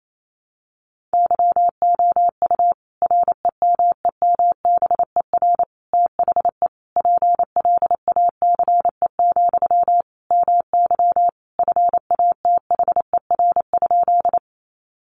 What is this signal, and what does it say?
Morse code, heard as a keyed tone, about 28 wpm, 700 Hz. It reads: YOU REMEMBER THE PLACE, MY FATHER?
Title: Sentences from Top 400 Words 28wpm